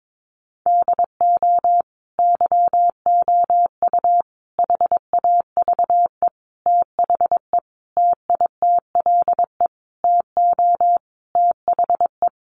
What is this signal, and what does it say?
Morse code, heard as a keyed tone, about 22 wpm, 700 Hz. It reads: DO YOU HAVE THE TITLE TO THE
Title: Sentences from Top 1200 Words 22wpm